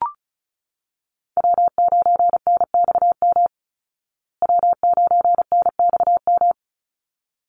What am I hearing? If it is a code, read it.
Morse code, heard as a keyed tone, about 34 wpm, 700 Hz. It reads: W9NXM W9NXM
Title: Call Signs Repeated 3x - 35wpm